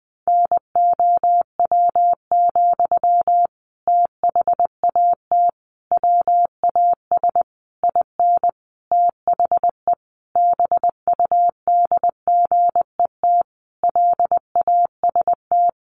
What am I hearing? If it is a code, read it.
Morse code, heard as a keyed tone, about 20 wpm, 700 Hz. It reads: NOW, THAT WAS IN THE BUDGET LAST
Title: Sentences from Top 1200 Words 20wpm